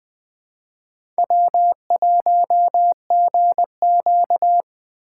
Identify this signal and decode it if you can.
Morse code, heard as a keyed tone, about 23 wpm, 700 Hz. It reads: W1GQ
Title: Speed Racing - USA 1x2 Call Signs 20wpm